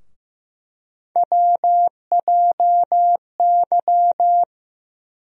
Morse code, heard as a keyed tone, about 15 wpm, 700 Hz. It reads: WJY
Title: USA Call Sign Postfixes - 3 Letters 15wpm